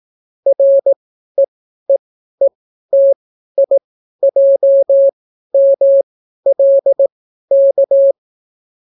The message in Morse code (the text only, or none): REEETIJMLK